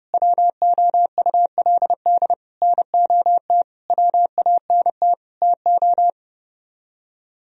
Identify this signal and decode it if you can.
Morse code, heard as a keyed tone, about 30 wpm, 700 Hz. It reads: WOULD NOT WANT TO